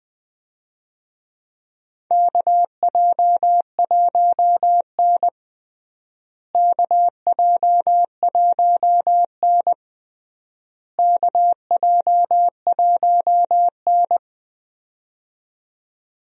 Morse code, heard as a keyed tone, about 20 wpm, 700 Hz. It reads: KJ1N KJ1N KJ1N
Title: USA Call Signs - 4 Characters - Repeated 3x - 20wpm